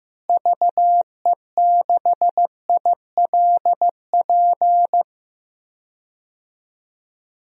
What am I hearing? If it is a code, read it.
Morse code, heard as a keyed tone, about 15 wpm, 700 Hz. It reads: VE6ILP
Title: DXCC-100 Call Signs - 6 Characters - Repeated 2x - 15wpm